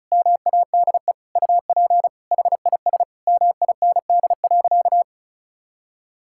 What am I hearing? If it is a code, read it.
Morse code, heard as a keyed tone, about 34 wpm, 700 Hz. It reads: MADE UP HIS MIND.
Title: Sentences from Top 1000 Words - Review Part 5 - 35wpm